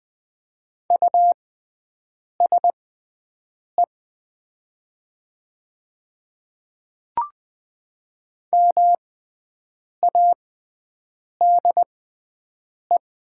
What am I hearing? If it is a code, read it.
Morse code, heard as a keyed tone, about 20 wpm, spaced as for 7 wpm, 700 Hz. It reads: USE MADE
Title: 6x Character Spacing - Top 200 Words - 20wpm